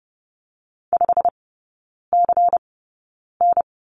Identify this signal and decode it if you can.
Morse code, heard as a keyed tone, about 30 wpm, spaced as for 9 wpm, 700 Hz. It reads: 5CN
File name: Lesson 16e - Letter c - Three Characters 30wpm